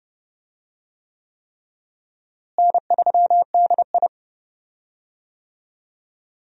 Morse code, heard as a keyed tone, about 32 wpm, 700 Hz. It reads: N3DI
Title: USA Call Signs - 4 Characters - Repeated 3x - 30wpm